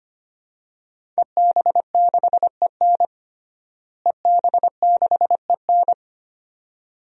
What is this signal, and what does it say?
Morse code, heard as a keyed tone, about 25 wpm, 700 Hz. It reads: EB6EN EB6EN
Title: DXCC-100 Call Signs - 5 Characters - Repeated 2x - 25wpm